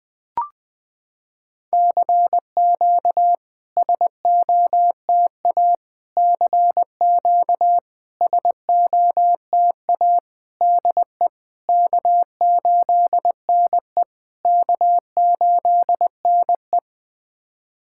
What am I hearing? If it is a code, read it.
Morse code, heard as a keyed tone, about 20 wpm, 660 Hz. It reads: CQ SOTA CQ SOTA DE K8NE K8NE
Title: SOTA QSO Practice 20wpm